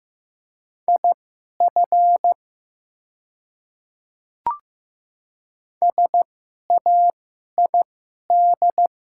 Morse code, heard as a keyed tone, about 15 wpm, spaced as for 11 wpm, 690 Hz. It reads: IF SAID
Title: 2x Character Spacing - Top 100 Words - 15wpm